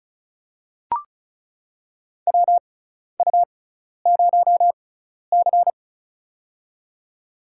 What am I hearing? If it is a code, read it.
Morse code, heard as a keyed tone, about 36 wpm, spaced as for 12 wpm, 700 Hz. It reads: WU0C
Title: USA Call Signs - 4 Characters - 6x Character Spacing - 35wpm